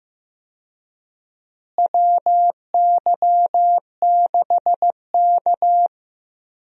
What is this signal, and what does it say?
Morse code, heard as a keyed tone, about 15 wpm, 700 Hz. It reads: WY6K
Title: USA Call Signs - 4 Characters - Repeated 3x - 15wpm